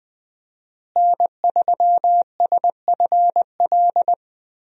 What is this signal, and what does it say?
Morse code, heard as a keyed tone, about 20 wpm, 700 Hz. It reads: N3SFL
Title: USA Call Signs - 5 Characters - Repeated 3x - 20wpm